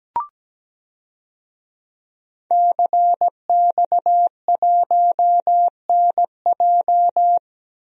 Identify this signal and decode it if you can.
Morse code, heard as a keyed tone, about 17 wpm, 700 Hz. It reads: CX1NJ
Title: DXCC-100 Call Signs - 5 Characters - Repeated 2x - 17wpm